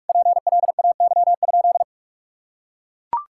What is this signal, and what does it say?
Morse code, heard as a keyed tone, about 41 wpm, 700 Hz. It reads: PLAY?
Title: Sentences from Top 1000 Words - Review Part 6 - 45wpm